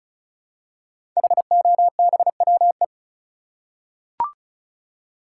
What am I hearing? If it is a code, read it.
Morse code, heard as a keyed tone, about 36 wpm, 700 Hz. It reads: HOBWE
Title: Code Groups - 5 Letters - 35wpm